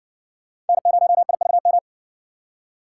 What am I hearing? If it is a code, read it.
Morse code, heard as a keyed tone, about 58 wpm, 700 Hz. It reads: N0IVM